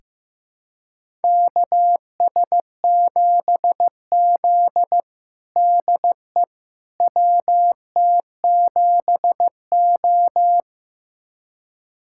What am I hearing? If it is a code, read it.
Morse code, heard as a keyed tone, about 15 wpm, 700 Hz. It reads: KS7Z DE WT7O
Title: USA Callsign DE Callsign - 4 Characters - 15wpm